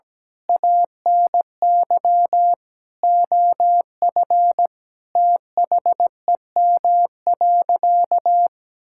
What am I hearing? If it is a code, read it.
Morse code, heard as a keyed tone, about 17 wpm, 700 Hz. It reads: ANY OF THEM.
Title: Sentences from Top 1000 Words - Review Part 6 - 17wpm